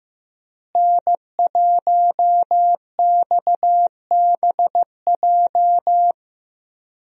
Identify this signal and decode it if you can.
Morse code, heard as a keyed tone, about 15 wpm, 700 Hz. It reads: N1XBJ
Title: USA Call Signs - 5 Characters - Repeated 2x - 15wpm